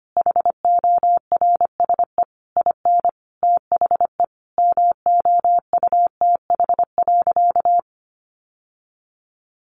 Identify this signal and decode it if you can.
Morse code, heard as a keyed tone, about 25 wpm, 700 Hz. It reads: HORSE IN THE MOUTH.
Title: Sentences from Top 1500 Words 25wpm